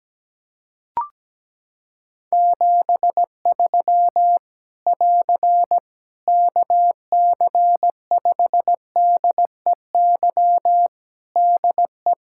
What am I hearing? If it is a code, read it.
Morse code, heard as a keyed tone, about 17 wpm, 700 Hz. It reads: 73 <AR> KC5DEY DE
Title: Practice QSO - Exchange 4 - Ending - 17wpm